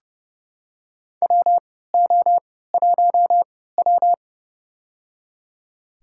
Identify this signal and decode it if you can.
Morse code, heard as a keyed tone, about 29 wpm, spaced as for 17 wpm, 700 Hz. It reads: WO1W
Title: USA Call Signs - 3x Character Spacing - 30wpm